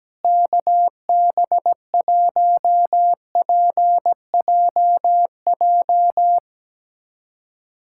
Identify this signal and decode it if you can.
Morse code, heard as a keyed tone, about 17 wpm, 700 Hz. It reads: KB1PJJ
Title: USA Callsign DE Callsign 17wpm